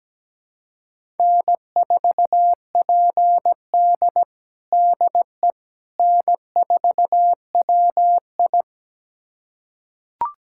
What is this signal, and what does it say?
Morse code, heard as a keyed tone, about 17 wpm, 700 Hz. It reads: N4PD DE N4WI
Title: USA Callsign DE Callsign - 4 Characters - 17wpm